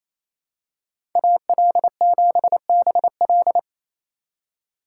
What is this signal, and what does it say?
Morse code, heard as a keyed tone, about 28 wpm, 700 Hz. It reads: AL7BL
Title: USA Call Signs - 5 Characters - 2x Character Spacing - 28wpm